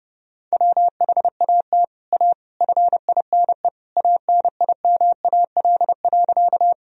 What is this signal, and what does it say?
Morse code, heard as a keyed tone, about 30 wpm, 700 Hz. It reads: WHAT A FINE ANIMAL.
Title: Sentences from Top 1000 Words - Review Part 1 - 30wpm